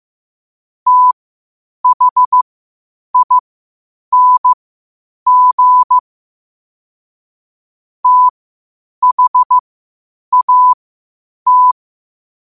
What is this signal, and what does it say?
Morse code, heard as a keyed tone, about 15 wpm, spaced as for 8 wpm, 1000 Hz. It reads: THING THAT
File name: qotdfiles / 8wpm